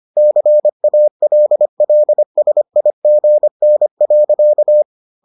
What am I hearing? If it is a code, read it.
Morse code, heard as a keyed tone, about 25 wpm, 600 Hz. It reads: CALLSIGN.